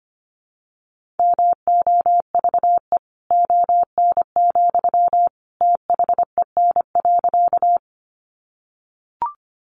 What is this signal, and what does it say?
Morse code, heard as a keyed tone, about 25 wpm, 700 Hz. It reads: MOVE ON, THEN.
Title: Sentences from Top 200 Words 25wpm